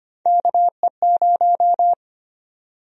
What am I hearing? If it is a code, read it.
Morse code, heard as a keyed tone, about 25 wpm, 700 Hz. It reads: KE0
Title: USA Call Sign Prefixes - 25wpm